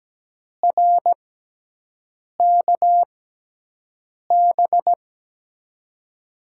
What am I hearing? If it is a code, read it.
Morse code, heard as a keyed tone, about 17 wpm, spaced as for 6 wpm, 700 Hz. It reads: RKB
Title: USA Call Signs - 6 Characters - 6x Character Spacing - 17wpm